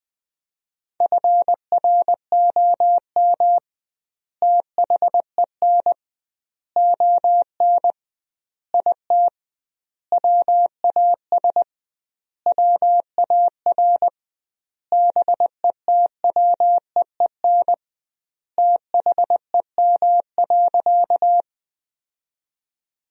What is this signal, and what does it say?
Morse code, heard as a keyed tone, about 20 wpm, 700 Hz. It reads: FROM THEN ON IT WAS WAR BETWEEN THEM.
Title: 2x Word-Spacing - Sentences from Top 400 Words - 20wpm